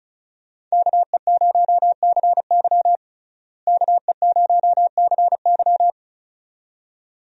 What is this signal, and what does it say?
Morse code, heard as a keyed tone, about 35 wpm, 700 Hz. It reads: KE0CY KE0CY
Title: USA Call Signs - 5 Characters - Repeated 2x - 35wpm